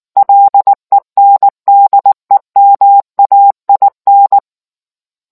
Morse code, heard as a keyed tone, about 19 wpm, 800 Hz. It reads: LENDEMAIN